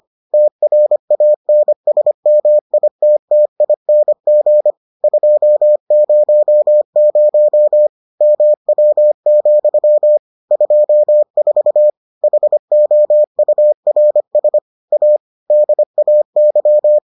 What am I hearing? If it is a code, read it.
Morse code, heard as a keyed tone, about 25 wpm, 600 Hz. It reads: TRANSMITTING 200 MW, 24 HOURS A DAY